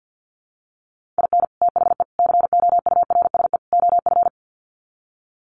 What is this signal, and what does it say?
Morse code, heard as a keyed tone, about 48 wpm, 700 Hz. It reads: IN THE COURSE OF
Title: Sets of 4 Words 50wpm